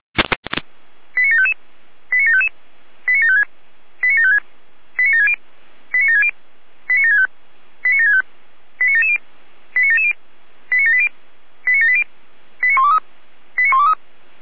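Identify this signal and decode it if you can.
Five-tone selective call (ZVEI), heard as quick runs of five tones, tones of about 70 ms. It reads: 89755 89755 89756 89756 89760 89760 89765 89765 89770 89770 89780 89780 89123 89123
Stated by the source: ZVEI five-tone call 89590 received by SDR.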